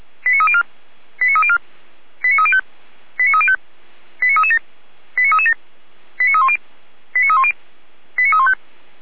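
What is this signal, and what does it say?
Five-tone selective call (ZVEI), heard as quick runs of five tones, tones of about 70 ms. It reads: 89304 89304 89305 89305 89337 89337 89310 89310 89315
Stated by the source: ZVEI five-tone call received by SDR, from the Feuerwehr Orlamünde audio archive.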